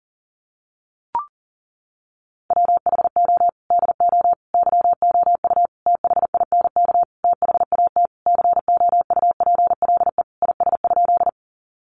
Morse code, heard as a keyed tone, about 40 wpm, 700 Hz. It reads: WHO DO YOU THINK THAT COUPLE IS?
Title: Sentences from Top 1100 Words 40wpm